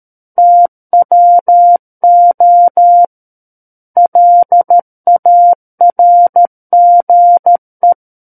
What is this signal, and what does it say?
Morse code, heard as a keyed tone, about 13 wpm, 700 Hz. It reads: TWO LARGE
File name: Book One / 04-Chapter 4 - 13 WPM